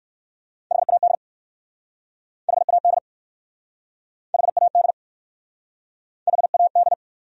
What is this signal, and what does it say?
Morse code, heard as a keyed tone, about 53 wpm, 700 Hz. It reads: HAD HAD HAD HAD
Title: Speed-Racing Top 200 Words 45wpm